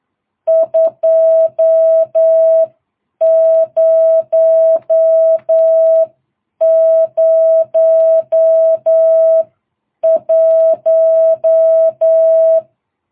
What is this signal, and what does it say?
Morse code, heard as a keyed tone, about 8 wpm, spaced as for 7 wpm, 660 Hz. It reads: 2001